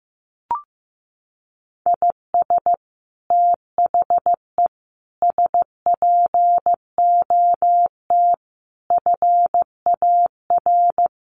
Morse code, heard as a keyed tone, about 14 wpm, 700 Hz. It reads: IS THE SPOT FAR
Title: Sentences from Top 1100 Words 15wpm